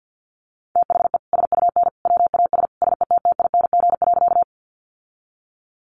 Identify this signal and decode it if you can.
Morse code, heard as a keyed tone, about 50 wpm, 700 Hz. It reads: THE SUN WAS SETTING.
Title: Sentences from Top 1300 Words 50wpm